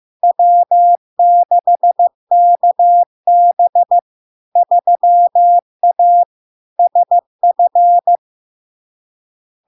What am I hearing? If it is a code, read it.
Morse code, heard as a keyed tone, about 15 wpm, 700 Hz. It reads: W6KB 3A SF
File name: LARK-FD-2022-Log-in-CW-15wpm-700-Hz